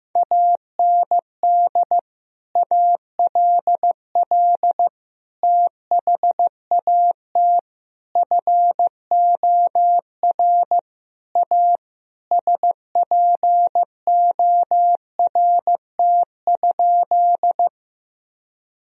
Morse code, heard as a keyed tone, about 15 wpm, 700 Hz. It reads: AND ALL THAT FOR A SPORT?
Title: Sentences from Top 1000 Words - Review Part 4 - 15wpm